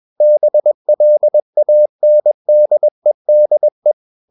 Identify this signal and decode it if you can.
Morse code, heard as a keyed tone, about 21 wpm, 600 Hz. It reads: BLANDEDE
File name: Ferie QRS 21wpm